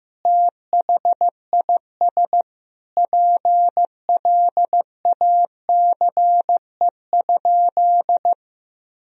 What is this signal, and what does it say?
Morse code, heard as a keyed tone, about 15 wpm, 700 Hz. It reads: THIS PLACE?
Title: Sentences from Top 1000 Words - Review Part 4 - 15wpm